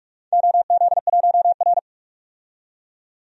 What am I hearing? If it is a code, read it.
Morse code, heard as a keyed tone, about 46 wpm, 700 Hz. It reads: OZ1R
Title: DXCC-100 Call Signs - 4 Characters - 45wpm